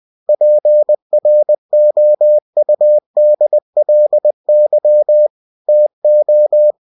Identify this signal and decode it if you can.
Morse code, heard as a keyed tone, about 20 wpm, 600 Hz. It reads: PROUDLY TO
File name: cw-books / winnie-the-pooh / winnie-20wpm-page-0099